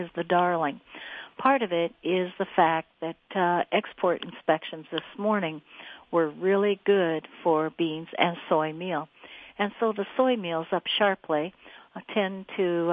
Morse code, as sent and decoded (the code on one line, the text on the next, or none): none